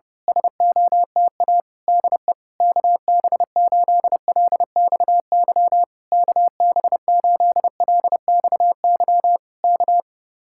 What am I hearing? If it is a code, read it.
Morse code, heard as a keyed tone, about 26 wpm, 700 Hz. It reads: SOTA DE KB8LXY KB8LXY K
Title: Speed-Racing Common QSO Phrases - 25wpm